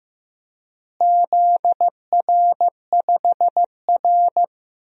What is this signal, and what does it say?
Morse code, heard as a keyed tone, about 16 wpm, 700 Hz. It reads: ZR5R
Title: DXCC-100 Call Signs - 4 Characters - Repeated 3x - 15wpm